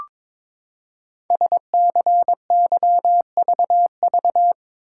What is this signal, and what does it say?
Morse code, heard as a keyed tone, about 22 wpm, 700 Hz. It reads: SCYVV
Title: Code Groups - 5 Letters - 22wpm